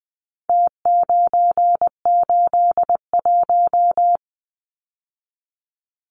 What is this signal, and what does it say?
Morse code, heard as a keyed tone, about 20 wpm, 700 Hz. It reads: T981
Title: Numbers - Contest Style - 4-Digit 20wpm